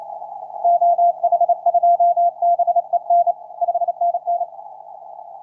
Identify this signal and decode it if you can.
Morse code, heard as a keyed tone, about 29 wpm, 690 Hz. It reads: OH2BEN 5NN
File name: jy9fc 23-11-16 2015utc 160m